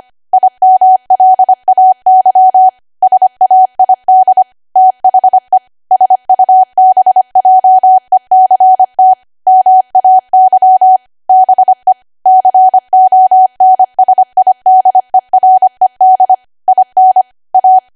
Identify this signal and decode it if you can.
Morse code, heard as a keyed tone, about 25 wpm, 750 Hz. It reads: IMLAY SAID THE SUBJECT MAY BE CONSIDERED IN A
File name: cwa / W1AW-Level3 / 25WPM